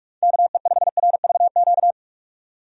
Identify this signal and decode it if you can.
Morse code, heard as a keyed tone, about 46 wpm, 700 Hz. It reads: KE5RVQ
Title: USA Call Signs - 6 Characters - 45wpm